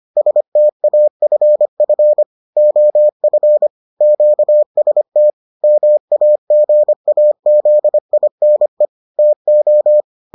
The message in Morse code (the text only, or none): STAFF OF QST MAGAZINE TO